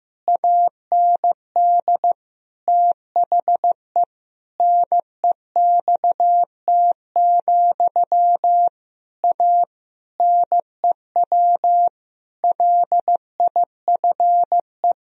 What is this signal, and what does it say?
Morse code, heard as a keyed tone, about 15 wpm, 700 Hz. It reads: AND THE NEXT, A NEW LIFE
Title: Sentences from Top 1000 Words - Review Part 1 - 15wpm